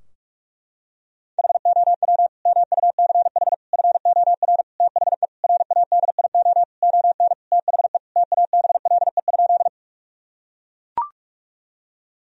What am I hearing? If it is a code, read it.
Morse code, heard as a keyed tone, about 41 wpm, 700 Hz. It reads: HOW MUCH FOR THE RADIO ON THE TABLE?
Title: Sentences from Top 1000 Words - Review Part 3 - 45wpm